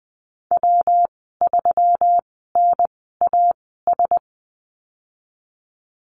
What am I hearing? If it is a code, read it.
Morse code, heard as a keyed tone, about 20 wpm, spaced as for 15 wpm, 700 Hz. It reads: W3NAS
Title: 154 Lesson 25g - Letter p - Call Signs 20wpm